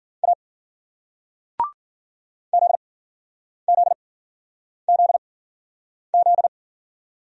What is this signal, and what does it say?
Morse code, heard as a keyed tone, about 52 wpm, 700 Hz. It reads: A Z Z Z Z